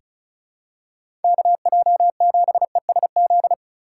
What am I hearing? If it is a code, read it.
Morse code, heard as a keyed tone, about 34 wpm, 700 Hz. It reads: KJ7ESZ
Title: USA Call Signs - 6 Characters - 3x Character Spacing - 35wpm